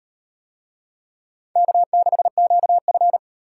Morse code, heard as a keyed tone, about 35 wpm, 700 Hz. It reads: K6QF